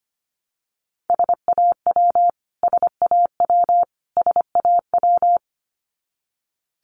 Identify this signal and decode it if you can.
Morse code, heard as a keyed tone, about 25 wpm, 700 Hz. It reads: SAW SAW SAW
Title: Top 500 Words Repeated 25wpm